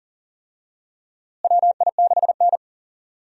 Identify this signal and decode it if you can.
Morse code, heard as a keyed tone, about 40 wpm, 700 Hz. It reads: WI6N
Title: USA Call Signs - 4 Characters - Repeated 3x - 40wpm